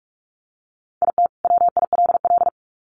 Morse code, heard as a keyed tone, about 44 wpm, 700 Hz. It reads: IT WILL
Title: Sets of 2 Words 45wpm